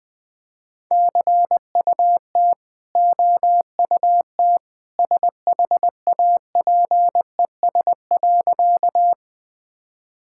Morse code, heard as a keyed tone, about 20 wpm, 700 Hz. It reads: CUT OUT SHAPES.
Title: Easy Sentences with Hints - 20wpm